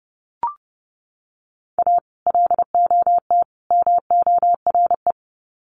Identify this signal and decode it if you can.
Morse code, heard as a keyed tone, about 30 wpm, 700 Hz. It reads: A LOT MORE
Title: Sets of 3 Words 30wpm